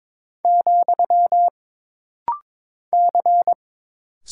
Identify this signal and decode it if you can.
Morse code, heard as a keyed tone, about 22 wpm, 700 Hz. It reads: , C
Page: This rapid-fire timing is geared to push your ICR (Instant Character Recognition) skills.